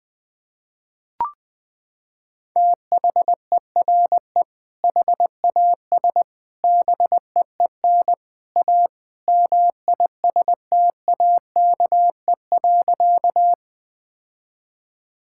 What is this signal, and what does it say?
Morse code, heard as a keyed tone, about 20 wpm, 700 Hz. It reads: THERE HAS BEEN A MISTAKE.
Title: Sentences with 3-Word Phrases - 20wpm